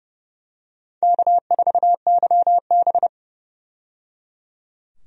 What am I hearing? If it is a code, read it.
Morse code, heard as a keyed tone, about 30 wpm, 700 Hz. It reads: K4YB
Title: USA Call Signs - 4 Characters - 30wpm